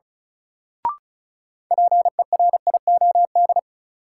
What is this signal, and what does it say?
Morse code, heard as a keyed tone, about 36 wpm, 700 Hz. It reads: PERIOD
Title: Sets of 5 Words - New Word Drill - 35wpm